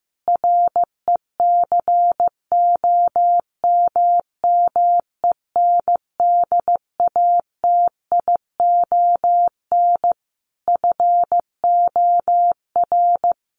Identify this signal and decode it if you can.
Morse code, heard as a keyed tone, about 15 wpm, 700 Hz. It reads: RECOMMENDATION FOR
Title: Sentences from Top 1400 Words 15wpm